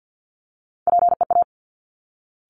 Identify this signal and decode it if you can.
Morse code, heard as a keyed tone, about 50 wpm, 700 Hz. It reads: PEA
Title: Top Three Letter Combinations in Words 50wpm